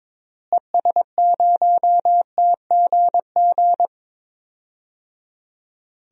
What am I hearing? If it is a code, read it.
Morse code, heard as a keyed tone, about 22 wpm, 700 Hz. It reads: ES0TGG
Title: DXCC-100 Call Signs - 6 Characters - Repeated 3x - 22wpm